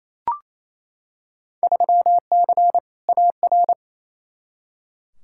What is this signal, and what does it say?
Morse code, heard as a keyed tone, about 28 wpm, 700 Hz. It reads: 3C AR
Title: Field Day Exchanges Encore - 28wpm